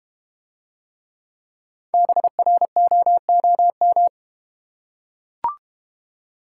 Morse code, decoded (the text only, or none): BROOM